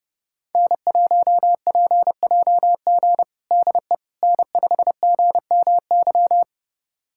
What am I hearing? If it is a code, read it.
Morse code, heard as a keyed tone, about 30 wpm, 700 Hz. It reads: N1PJG DE N5GMY
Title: USA Callsign DE Callsign - 5 Characters - 30wpm